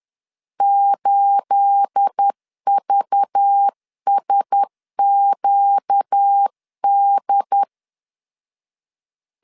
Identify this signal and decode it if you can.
Morse code, heard as a keyed tone, about 10 wpm, 800 Hz. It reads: 8VSQD
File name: LEZIONI / 9 28 10wpm 2842